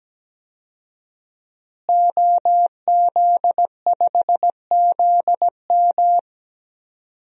Morse code, heard as a keyed tone, about 17 wpm, 700 Hz. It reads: OZ5ZM
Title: DXCC-100 Call Signs - 5 Characters - Repeated 3x - 17wpm